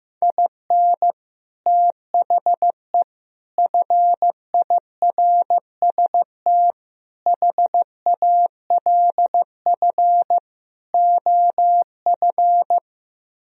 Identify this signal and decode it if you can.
Morse code, heard as a keyed tone, about 15 wpm, 700 Hz. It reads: IN THE FIRST HALF OF
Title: Speed-Racing - Sets of 5 Words 15wpm